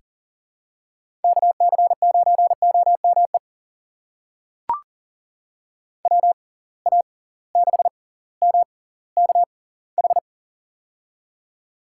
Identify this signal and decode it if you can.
Morse code, heard as a keyed tone, about 40 wpm, spaced as for 20 wpm, 700 Hz. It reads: KC9OME WA6MKH
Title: USA Call Signs - 6 Characters - 6x Character Spacing - 40wpm